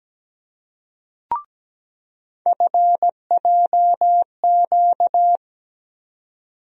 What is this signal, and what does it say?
Morse code, heard as a keyed tone, about 17 wpm, 700 Hz. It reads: FJQ
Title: USA Call Sign Postfixes - 3 Letters 17wpm